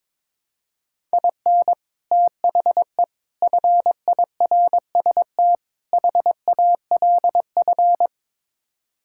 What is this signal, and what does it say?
Morse code, heard as a keyed tone, about 22 wpm, 700 Hz. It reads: IN THE FIRST HALF
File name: practice / 22wpm / Build A Phrase - 5-Word Phrases - 22wpm